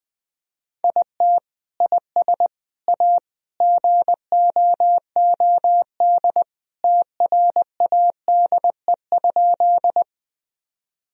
Morse code, heard as a keyed tone, about 20 wpm, 700 Hz. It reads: IT IS A GOOD TRADE?
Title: Sentences from Top 1000 Words - Review Part 4 - 20wpm